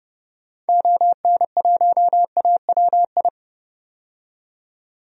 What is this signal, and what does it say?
Morse code, heard as a keyed tone, about 32 wpm, 700 Hz. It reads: ON1AWI
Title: DXCC-100 Call Signs - 6 Characters - 30wpm